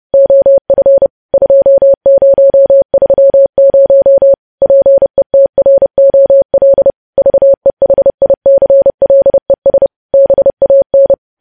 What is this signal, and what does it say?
Morse code, heard as a keyed tone, about 30 wpm, 550 Hz. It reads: OF 2030 PETROL VEHICLES BAN
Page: news headlines in morse code at 30 wpm